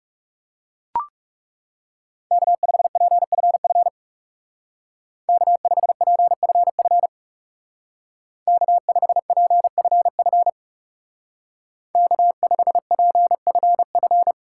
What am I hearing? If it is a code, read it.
Morse code, heard as a keyed tone, about 37 wpm, 700 Hz. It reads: K5PFF K5PFF K5PFF K5PFF
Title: Speed Racing - USA Call Signs 30wpm